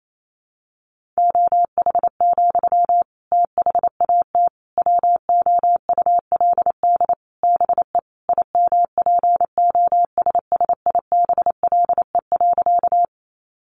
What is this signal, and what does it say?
Morse code, heard as a keyed tone, about 28 wpm, 700 Hz. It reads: OH, THAT WOULD BE IMPOSSIBLE.
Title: Sentences from Top 1100 Words 28wpm